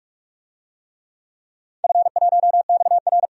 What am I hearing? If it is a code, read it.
Morse code, heard as a keyed tone, about 41 wpm, 700 Hz. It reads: F1XR
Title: DXCC-100 Call Signs - 4 Characters - 45wpm